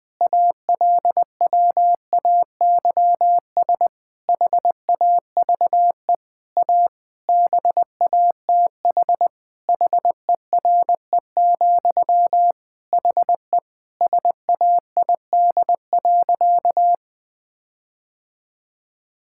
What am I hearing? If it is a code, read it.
Morse code, heard as a keyed tone, about 20 wpm, 700 Hz. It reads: ALWAYS HAVE A BATH HERE, HE SAID.
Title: Sentences from Top 1600 Words 20wpm